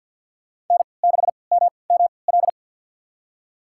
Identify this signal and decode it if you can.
Morse code, heard as a keyed tone, about 52 wpm, spaced as for 29 wpm, 700 Hz. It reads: N6MML
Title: USA Call Signs - 5 Characters - 3x Character Spacing - 50wpm